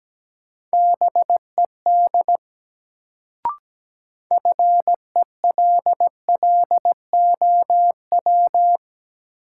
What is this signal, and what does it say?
Morse code, heard as a keyed tone, about 17 wpm, 700 Hz. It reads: BED FELLOW
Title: Sets of 5 Words Encore - New Word Drill - 17wpm